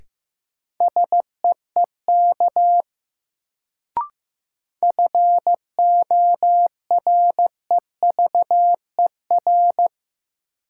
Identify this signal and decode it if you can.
Morse code, heard as a keyed tone, about 15 wpm, 700 Hz. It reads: SEEK FOREVER
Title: Sentences from Top 1600 Words - New Word Drill Alt - 15wpm